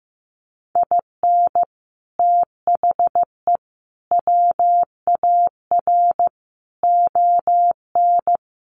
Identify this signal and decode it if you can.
Morse code, heard as a keyed tone, about 15 wpm, 700 Hz. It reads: IN THE WAR ON
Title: Sets of 5 Words Encore2 15wpm